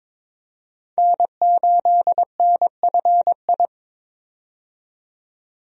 Morse code, heard as a keyed tone, about 21 wpm, 700 Hz. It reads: N8NFI
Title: USA Call Signs - 5 Characters - 22wpm